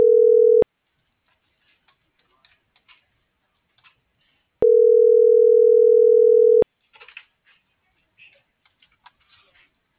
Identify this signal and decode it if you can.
Ringback tone (North America), 440 + 480 Hz.